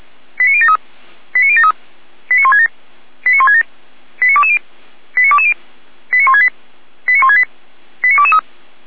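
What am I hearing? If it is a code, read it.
Five-tone selective call (ZVEI), heard as quick runs of five tones, tones of about 70 ms. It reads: 89073 89073 89167 89167 89220 89220 89168 89168 89223
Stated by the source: ZVEI five-tone selective call 89011 received with an SDR.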